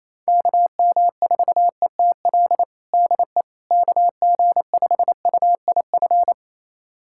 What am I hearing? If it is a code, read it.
Morse code, heard as a keyed tone, about 28 wpm, 700 Hz. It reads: KM4ETL DE KG5UIF
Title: USA Callsign DE Callsign - 6 Characters - 28wpm